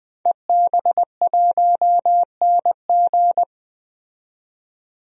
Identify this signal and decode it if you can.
Morse code, heard as a keyed tone, about 20 wpm, 700 Hz. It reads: EB1NG
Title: DXCC-100 Call Signs - 5 Characters - 20wpm